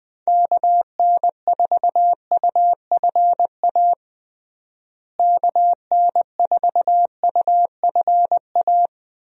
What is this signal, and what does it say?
Morse code, heard as a keyed tone, about 21 wpm, 700 Hz. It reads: KN4UFA KN4UFA
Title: USA Call Signs - 6 Characters - Repeated 3x - 20wpm